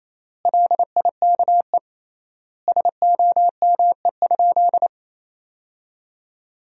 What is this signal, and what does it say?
Morse code, heard as a keyed tone, about 27 wpm, 700 Hz. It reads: LIKE SOME?
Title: 3x Word-Spacing - Sentences from Top 100 Words - 28wpm